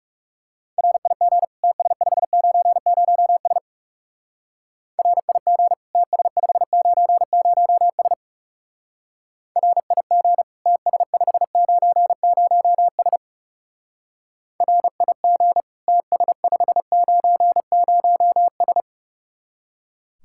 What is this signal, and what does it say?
Morse code, heard as a keyed tone, about 37 wpm, 700 Hz. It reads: RIG TS590S RIG TS590S RIG TS590S RIG TS590S
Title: Speed-Racing Common QSO Phrases Encore - 30wpm